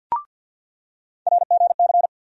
Morse code, heard as a keyed tone, about 47 wpm, 700 Hz. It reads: RGC
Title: USA Call Sign Postfixes - 3 Letters 50wpm